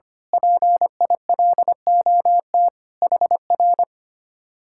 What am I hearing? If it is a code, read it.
Morse code, heard as a keyed tone, about 25 wpm, 700 Hz. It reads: PILOT HR
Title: Speed-Racing Common QSO Phrases Encore - 25wpm